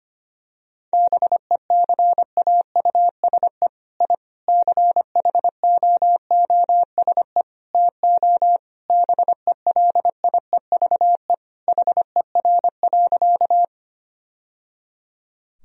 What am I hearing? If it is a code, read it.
Morse code, heard as a keyed tone, about 25 wpm, 700 Hz. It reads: BECAUSE I CHOOSE TO BELIEVE HER.
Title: Sentences from Top 1000 Words - Review Part 4 - 25wpm